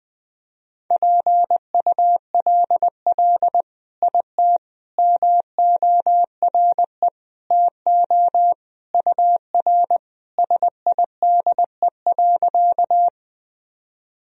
Morse code, heard as a keyed tone, about 21 wpm, 700 Hz. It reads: PULL IT MORE TO UR SIDE.
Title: Sentences from Top 400 Words and Abbreviations 20wpm